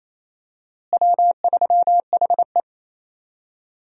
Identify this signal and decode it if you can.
Morse code, heard as a keyed tone, about 28 wpm, 700 Hz. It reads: W3HE
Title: USA Call Signs - 4 Characters - Repeated 3x - 28wpm